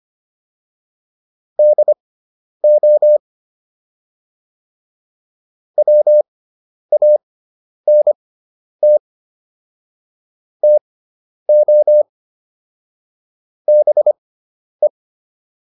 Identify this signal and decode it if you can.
Morse code, heard as a keyed tone, about 25 wpm, spaced as for 10 wpm, 600 Hz. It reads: DO WANT TO BE
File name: WD401 10